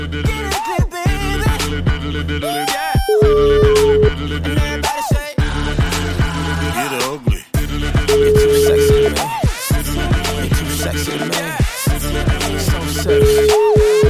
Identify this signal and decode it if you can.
Ringback tone (Europe, 425 Hz), with noise.